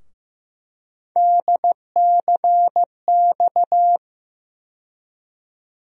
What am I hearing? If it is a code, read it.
Morse code, heard as a keyed tone, about 15 wpm, 700 Hz. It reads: DCX
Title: USA Call Sign Postfixes - 3 Letters 15wpm